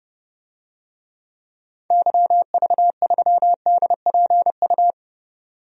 Morse code, heard as a keyed tone, about 29 wpm, 700 Hz. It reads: YV3DPU
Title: DXCC-100 Call Signs - 6 Characters - 30wpm